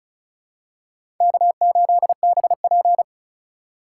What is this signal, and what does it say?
Morse code, heard as a keyed tone, about 36 wpm, 700 Hz. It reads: K8BP
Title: USA Call Signs - 4 Characters - 3x Character Spacing - 35wpm